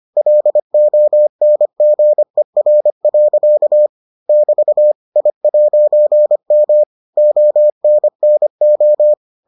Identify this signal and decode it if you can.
Morse code, heard as a keyed tone, about 25 wpm, 600 Hz. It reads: LONGER. <BT> I'M ONNO